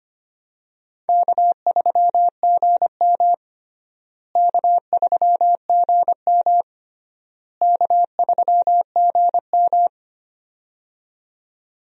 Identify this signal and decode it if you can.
Morse code, heard as a keyed tone, about 25 wpm, 700 Hz. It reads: K3GM K3GM K3GM
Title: USA Call Signs - 4 Characters - Repeated 3x - 25wpm